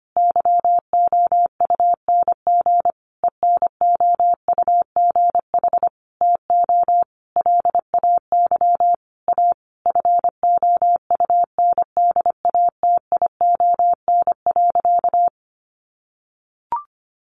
Morse code, heard as a keyed tone, about 25 wpm, 700 Hz. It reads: YOUNG ENOUGH TO LAY A FOUNDATION.
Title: Sentences from Top 1200 Words 25wpm